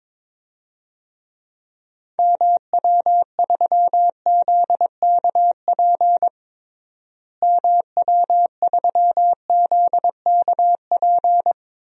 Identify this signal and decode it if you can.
Morse code, heard as a keyed tone, about 22 wpm, 700 Hz. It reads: MW3ZKP MW3ZKP
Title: DXCC-100 Call Signs - 6 Characters - Repeated 2x - 22wpm